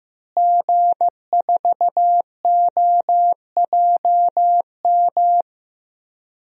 Morse code, heard as a keyed tone, about 15 wpm, 700 Hz. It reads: G4OJM
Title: DXCC-100 Call Signs - 5 Characters - Repeated 3x - 15wpm